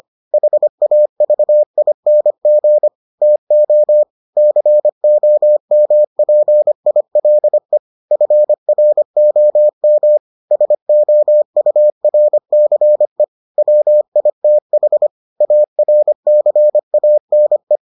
Morse code, heard as a keyed tone, about 25 wpm, 600 Hz. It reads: HAVING TO COMPILE FROM SOURCE WITH ARCANE